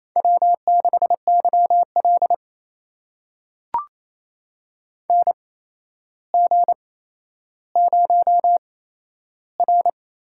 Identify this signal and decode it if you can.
Morse code, heard as a keyed tone, about 28 wpm, spaced as for 12 wpm, 700 Hz. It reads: W6YL NG0R
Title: USA Call Signs - 4 Characters - 8x Character Spacing - 28wpm